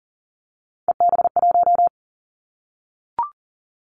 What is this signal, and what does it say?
Morse code, heard as a keyed tone, about 40 wpm, 700 Hz. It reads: EB1
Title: Callsign Prefixes for Top 25 DXCC - Repeated 3x - 40wpm